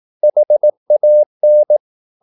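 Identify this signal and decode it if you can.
Morse code, heard as a keyed tone, about 18 wpm, 600 Hz. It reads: HAN
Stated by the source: Slow (QRS) Morse practice at 18 wpm, CW med Gnister.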